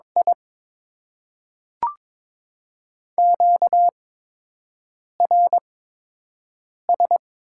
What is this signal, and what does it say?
Morse code, heard as a keyed tone, about 21 wpm, spaced as for 6 wpm, 700 Hz. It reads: I QRS
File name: Top 90 QSO Elements - 8x Character Spacing - 22wpm